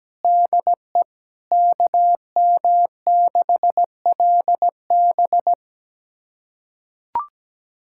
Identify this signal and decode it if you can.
Morse code, heard as a keyed tone, about 17 wpm, 700 Hz. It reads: DE KM6LB
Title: USA Callsign DE Callsign 17wpm